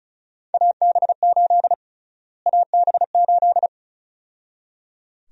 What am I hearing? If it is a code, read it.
Morse code, heard as a keyed tone, about 34 wpm, 700 Hz. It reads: AB8 AB8
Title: USA Call Sign Prefixes - Repeated 2x - 35wpm